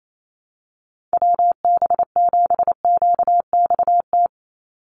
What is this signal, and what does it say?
Morse code, heard as a keyed tone, about 28 wpm, 700 Hz. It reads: WB7QXT
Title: USA Call Signs 28wpm